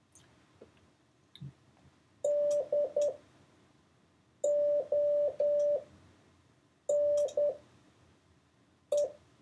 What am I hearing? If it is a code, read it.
Morse code, heard as a keyed tone, about 10 wpm, spaced as for 5 wpm, 580 Hz. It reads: DONE
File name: 4of6-words-1